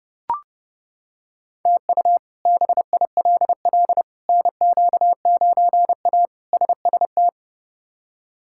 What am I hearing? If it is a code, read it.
Morse code, heard as a keyed tone, about 32 wpm, 700 Hz. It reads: TU BILL NQ9A SST
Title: K1USN SST QSO Practice - 30wpm